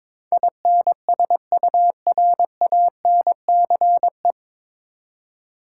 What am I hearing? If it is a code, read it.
Morse code, heard as a keyed tone, about 22 wpm, 700 Hz. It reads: INSURANCE
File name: 9 Letter Words - First 3 Letters Given - 22wpm